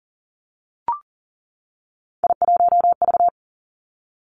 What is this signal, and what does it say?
Morse code, heard as a keyed tone, about 38 wpm, 700 Hz. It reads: I1V
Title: Three Challenging Letter-Number Combinations 40wpm